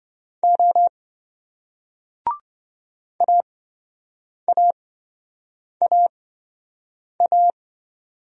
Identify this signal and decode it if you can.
Morse code, heard as a keyed tone, about 27 wpm, spaced as for 7 wpm, 700 Hz. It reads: O AAAA